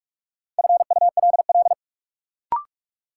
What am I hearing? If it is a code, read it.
Morse code, heard as a keyed tone, about 41 wpm, 700 Hz. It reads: FULL
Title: Sentences from Top 500 Words - New Word Drill - 45wpm